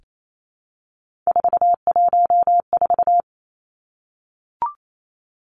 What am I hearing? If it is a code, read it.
Morse code, heard as a keyed tone, about 28 wpm, 700 Hz. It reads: 414
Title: Numbers - 3-Digit 28wpm